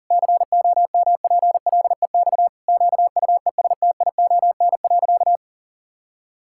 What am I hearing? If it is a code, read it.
Morse code, heard as a keyed tone, about 39 wpm, 700 Hz. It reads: COMPLEX QUESTION.
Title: Sentences from Top 1000 Words - Review Part 6 - 40wpm